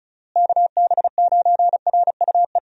Morse code, heard as a keyed tone, about 36 wpm, 700 Hz. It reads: KB9RUE
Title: USA Call Signs - 6 Characters - Repeated 3x - 35wpm